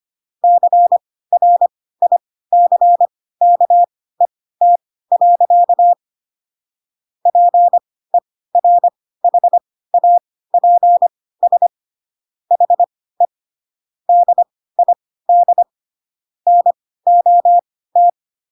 Morse code, heard as a keyed tone, about 25 wpm, spaced as for 16 wpm, 700 Hz. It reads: CRICKET. PERHAPS HE DID NOT
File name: Pinocchio / Pinocchio ch4 10 25 ik1hge